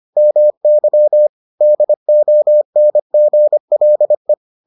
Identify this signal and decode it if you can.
Morse code, heard as a keyed tone, about 25 wpm, 600 Hz. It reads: MY DONGLE